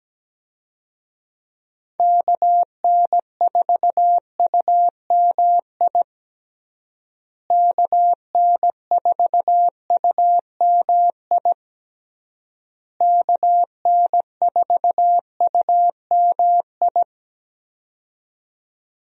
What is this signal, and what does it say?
Morse code, heard as a keyed tone, about 17 wpm, 700 Hz. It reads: KN4UMI KN4UMI KN4UMI
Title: USA Call Signs - 6 Characters - Repeated 3x - 17wpm